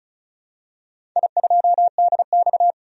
Morse code, heard as a keyed tone, about 34 wpm, 700 Hz. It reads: I2DX
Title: DXCC-100 Call Signs - 4 Characters - 35wpm